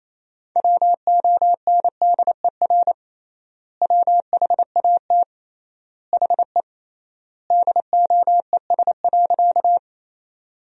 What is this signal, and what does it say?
Morse code, heard as a keyed tone, about 28 wpm, 700 Hz. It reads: WONDER WHAT HE DOES.
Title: 3x Word-Spacing - Sentences from Top 500 Words - 28wpm